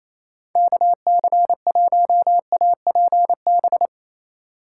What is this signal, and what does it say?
Morse code, heard as a keyed tone, about 28 wpm, 700 Hz. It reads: KC1APB
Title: USA Call Signs - 2x Character Spacing - 28wpm